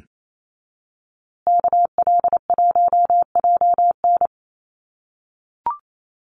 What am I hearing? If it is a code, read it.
Morse code, heard as a keyed tone, about 28 wpm, 700 Hz. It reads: KL1JN
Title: USA Call Signs Repeated 3x 28wpm